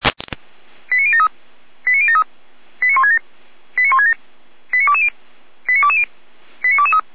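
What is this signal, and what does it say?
Five-tone selective call (ZVEI), heard as quick runs of five tones, tones of about 70 ms. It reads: 89073 89073 89167 89167 89220 89220 89223
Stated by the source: ZVEI five-tone selective call 89001 received on an SDR, Feuerwehr Orlamünde fire brigade.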